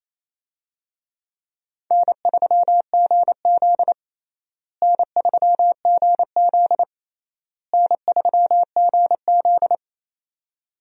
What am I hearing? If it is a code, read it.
Morse code, heard as a keyed tone, about 27 wpm, 700 Hz. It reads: N3GZ N3GZ N3GZ
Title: USA Call Signs - 4 Characters - Repeated 3x - 28wpm